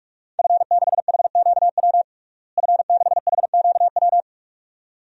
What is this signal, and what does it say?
Morse code, heard as a keyed tone, about 42 wpm, 700 Hz. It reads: F6HQW F6HQW
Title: DXCC-100 Call Signs - 5 Characters - Repeated 2x - 45wpm